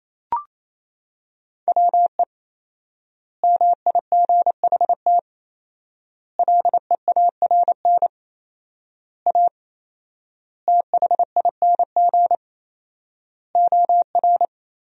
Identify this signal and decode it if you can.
Morse code, heard as a keyed tone, about 28 wpm, 700 Hz. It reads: WE MIGHT LEARN A THING OR
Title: 4x Word-Spacing - Sentences from Top 300 Words - 28wpm